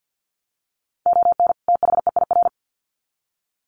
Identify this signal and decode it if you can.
Morse code, heard as a keyed tone, about 53 wpm, 700 Hz. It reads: ON THEIR
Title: Sets of 2 Words 50wpm